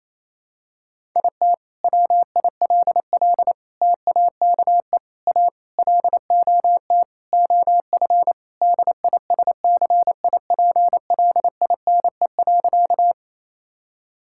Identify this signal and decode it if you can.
Morse code, heard as a keyed tone, about 28 wpm, 700 Hz. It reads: IT WILL TAKE A LOT OF DISCIPLINE.
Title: Sentences from Top 1000 Words - Review Part 6 - 28wpm